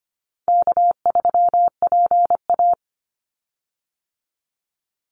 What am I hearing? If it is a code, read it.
Morse code, heard as a keyed tone, about 25 wpm, 700 Hz. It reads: K3PA
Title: Call Signs Repeated 2x - 25wpm